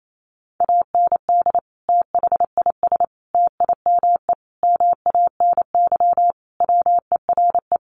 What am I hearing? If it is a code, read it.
Morse code, heard as a keyed tone, about 28 wpm, 700 Hz. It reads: AND THIS TIME MANY WERE
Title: Sentences from Top 1400 Words 28wpm